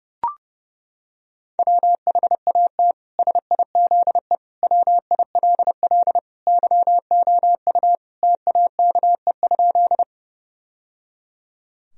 Morse code, heard as a keyed tone, about 30 wpm, 700 Hz. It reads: WHAT SIZE WILL YOU TAKE?
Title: Sentences from Top 1000 Words - Review Part 2 - 30wpm